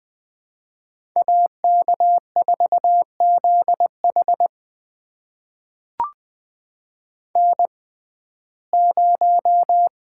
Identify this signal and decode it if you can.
Morse code, heard as a keyed tone, about 20 wpm, spaced as for 14 wpm, 700 Hz. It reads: AK4ZH N0
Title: USA Call Signs - 5 Characters - 6x Character Spacing - 20wpm